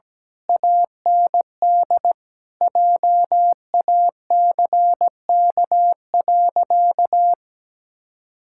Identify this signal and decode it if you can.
Morse code, heard as a keyed tone, about 17 wpm, 700 Hz. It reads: AND JACK.
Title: Pangram Challenge v2 - 17wpm